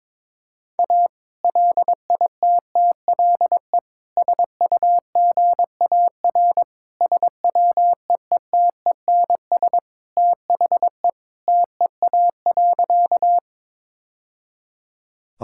Morse code, heard as a keyed tone, about 21 wpm, 700 Hz. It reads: A LITTLE SUGAR SWEETENS THE TEA.
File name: practice / 22wpm / Sentences with 2-Word Phrases - 22wpm